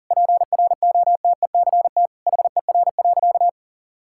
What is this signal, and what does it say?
Morse code, heard as a keyed tone, about 40 wpm, 700 Hz. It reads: PROTECT HER.
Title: Sentences from Top 1600 Words 40wpm